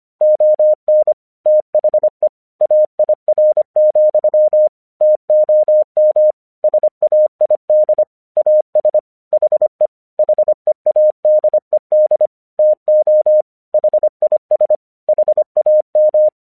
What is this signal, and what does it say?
Morse code, heard as a keyed tone, about 24 wpm, 610 Hz. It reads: ON THE AIR, TOM SAID AS HE HEADED TO HIS HAM
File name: 1 English 25wpm / AQC05